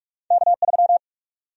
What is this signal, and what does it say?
Morse code, heard as a keyed tone, about 46 wpm, 700 Hz. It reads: K3
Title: USA Call Sign Prefixes - 45wpm